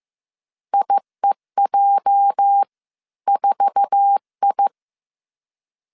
Morse code, heard as a keyed tone, about 15 wpm, spaced as for 13 wpm, 800 Hz. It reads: IEJ4I
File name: CW / DET / 1 9 1500C S A 14wpm 3222